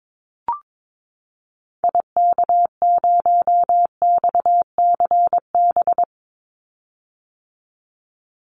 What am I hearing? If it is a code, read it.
Morse code, heard as a keyed tone, about 22 wpm, 700 Hz. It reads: IK0XCB
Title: Call Signs - 22wpm